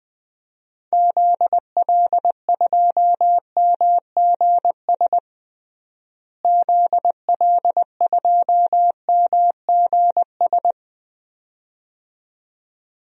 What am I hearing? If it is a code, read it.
Morse code, heard as a keyed tone, about 20 wpm, 700 Hz. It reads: ZL2MGS ZL2MGS
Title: DXCC-100 Call Signs - 6 Characters - Repeated 2x - 20wpm